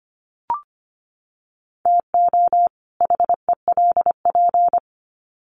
Morse code, heard as a keyed tone, about 25 wpm, 700 Hz. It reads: TO HELP
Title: Sets of 2 Words 25wpm